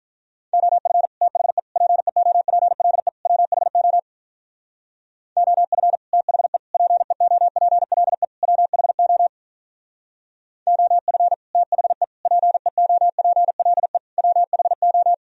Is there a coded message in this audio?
Yes: Morse code from 0.5 to 15.1 s, heard as a keyed tone, about 45 wpm, 700 Hz. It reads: OF THE PEOPLE WHO OF THE PEOPLE WHO OF THE PEOPLE WHO